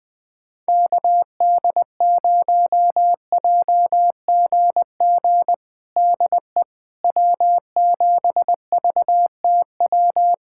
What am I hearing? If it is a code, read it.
Morse code, heard as a keyed tone, about 20 wpm, 700 Hz. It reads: KD0JGG DE W7VTW
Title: USA Callsign DE Callsign 20wpm